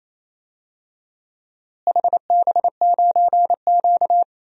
Morse code, heard as a keyed tone, about 28 wpm, 700 Hz. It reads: HB9Q
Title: DXCC-100 Call Signs - 4 Characters - Repeated 2x - 28wpm